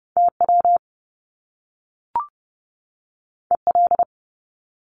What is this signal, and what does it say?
Morse code, heard as a keyed tone, about 30 wpm, 700 Hz. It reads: TW EL
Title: Top Two Letter Combinations in Words 30wpm